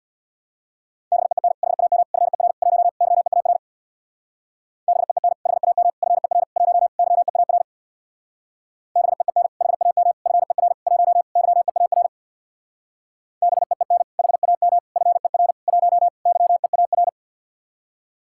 Morse code, heard as a keyed tone, about 65 wpm, 700 Hz. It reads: BEEN HAM FER 1 YEAR BEEN HAM FER 1 YEAR BEEN HAM FER 1 YEAR BEEN HAM FER 1 YEAR